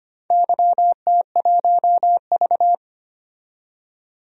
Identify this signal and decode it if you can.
Morse code, heard as a keyed tone, about 25 wpm, 700 Hz. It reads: YT1V
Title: DXCC-100 Call Signs - 4 Characters - 25wpm